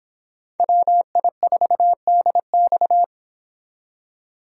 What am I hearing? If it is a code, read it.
Morse code, heard as a keyed tone, about 29 wpm, 700 Hz. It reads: WI4DX